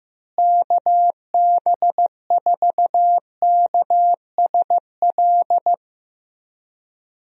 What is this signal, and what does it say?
Morse code, heard as a keyed tone, about 15 wpm, 700 Hz. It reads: KB4KSL
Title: USA Call Signs - 6 Characters - Repeated 3x - 15wpm